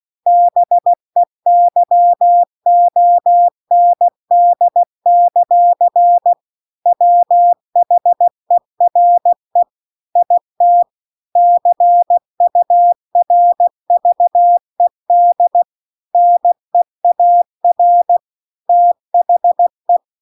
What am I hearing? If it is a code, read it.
Morse code, heard as a keyed tone, about 16 wpm, 700 Hz. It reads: BEYOND; WHERE IT CURVED NEAR THE
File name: CW-The-Big-Twoheaded-River-2-16WPM